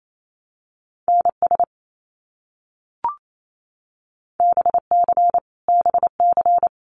Morse code, heard as a keyed tone, about 28 wpm, 700 Hz. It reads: NS BC BC
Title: Canadian Provinces and Territories - Repeated 3x - 28wpm